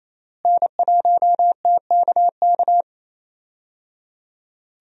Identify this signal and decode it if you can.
Morse code, heard as a keyed tone, about 28 wpm, 700 Hz. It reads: N1TKK
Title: USA Call Signs - 5 Characters - Repeated 3x - 28wpm